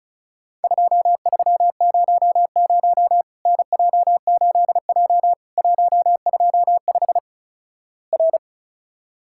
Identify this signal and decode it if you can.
Morse code, heard as a keyed tone, about 36 wpm, 700 Hz. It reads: 2300 NJ8J 125 R
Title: Worked All Europe DX Contest - QTC Exchange - 35wpm